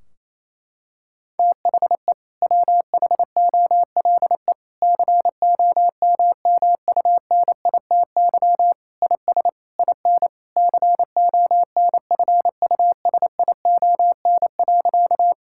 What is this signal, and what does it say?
Morse code, heard as a keyed tone, about 28 wpm, 700 Hz. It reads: THE WHOLE COMMUNITY IS IN CONFUSION.
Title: Sentences from Top 1600 Words 28wpm